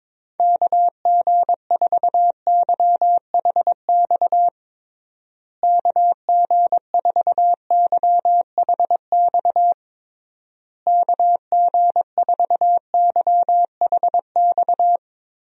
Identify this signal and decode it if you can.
Morse code, heard as a keyed tone, about 22 wpm, 700 Hz. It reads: KG4YHX KG4YHX KG4YHX
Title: USA Call Signs - 6 Characters - Repeated 3x - 22wpm